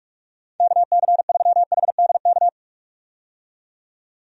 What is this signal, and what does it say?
Morse code, heard as a keyed tone, about 42 wpm, 700 Hz. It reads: KC3HDK
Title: USA Call Signs - 6 Characters - 45wpm